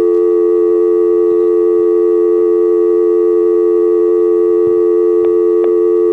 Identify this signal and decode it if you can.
Dial tone (North America), 350 + 440 Hz.